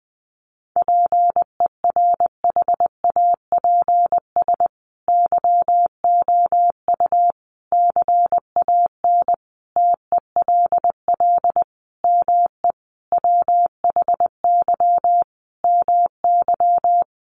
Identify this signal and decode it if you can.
Morse code, heard as a keyed tone, about 20 wpm, 700 Hz. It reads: PERHAPS YOU CAN TELL ME WHY MY
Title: Sentences from Top 1200 Words 20wpm